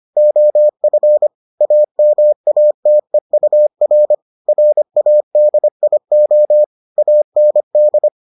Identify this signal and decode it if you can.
Morse code, heard as a keyed tone, about 25 wpm, 600 Hz. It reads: OF AMATEUR RADIO AND